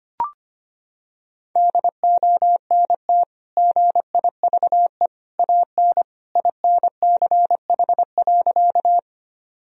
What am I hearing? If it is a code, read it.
Morse code, heard as a keyed tone, about 25 wpm, 700 Hz. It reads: DONT GIVE AN INCH.
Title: Sentences from Top 1000 Words - Review Part 3 - 25wpm